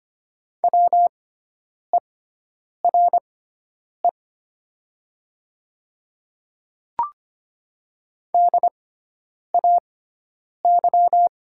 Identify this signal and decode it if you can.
Morse code, heard as a keyed tone, about 25 wpm, spaced as for 9 wpm, 700 Hz. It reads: WERE DAY
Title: 6x Character Spacing - Top 100 Words - 25wpm